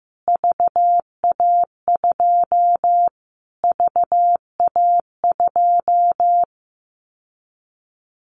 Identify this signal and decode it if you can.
Morse code, heard as a keyed tone, about 15 wpm, 700 Hz. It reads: VA2 VA2
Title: Callsign Prefixes for Top 25 DXCC - Repeated 3x - 15wpm